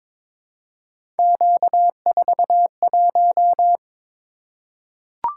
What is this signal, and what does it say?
Morse code, heard as a keyed tone, about 23 wpm, 700 Hz. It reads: Q41
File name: practice / 22wpm / Three Letter-Number Combinations - 22wpm